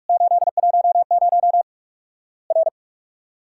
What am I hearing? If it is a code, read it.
Morse code, heard as a keyed tone, about 47 wpm, 700 Hz. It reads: 810 R
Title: Worked All Europe DX Contest - QTC Exchange - 45wpm